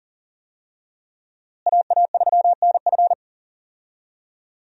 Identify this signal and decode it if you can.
Morse code, heard as a keyed tone, about 40 wpm, 700 Hz. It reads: AA3NF
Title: USA Call Signs - 5 Characters - Repeated 3x - 40wpm